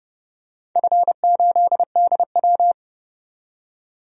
Morse code, heard as a keyed tone, about 30 wpm, 700 Hz. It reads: F8DW
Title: DXCC-100 Call Signs - 4 Characters - 30wpm